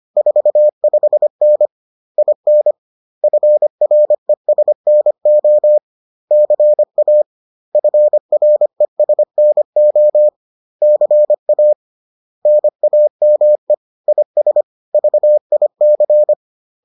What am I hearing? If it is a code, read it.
Morse code, heard as a keyed tone, about 25 wpm, 600 Hz. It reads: 45N IN FRESNO CA FRESNO CA NAME IS VIC